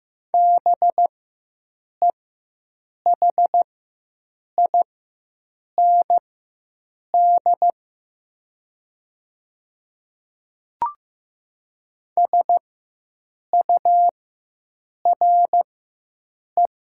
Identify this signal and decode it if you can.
Morse code, heard as a keyed tone, about 15 wpm, spaced as for 7 wpm, 700 Hz. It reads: BEHIND SURE
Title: 4x Character Spacing - Top 500 Words - 15wpm